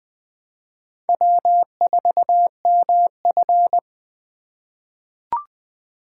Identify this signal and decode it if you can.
Morse code, heard as a keyed tone, about 20 wpm, 700 Hz. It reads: W4MF
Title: USA Call Signs - 4 Characters - 3x Character Spacing - 20wpm